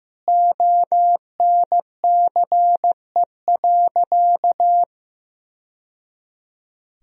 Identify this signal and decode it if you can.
Morse code, heard as a keyed tone, about 15 wpm, 700 Hz. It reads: ONCE.
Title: Sentences from Top 1000 Words - Review Part 1 - 15wpm